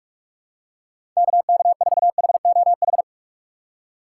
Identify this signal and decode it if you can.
Morse code, heard as a keyed tone, about 47 wpm, 700 Hz. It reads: KK4HOH
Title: USA Call Signs - 6 Characters - 6x Character Spacing - 45wpm